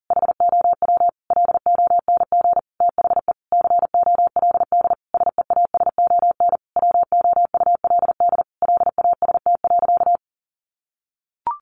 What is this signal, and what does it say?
Morse code, heard as a keyed tone, about 40 wpm, 700 Hz. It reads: HOW LONG THE COLD SEASON WOULD LAST.
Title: Sentences from Top 1400 Words 40wpm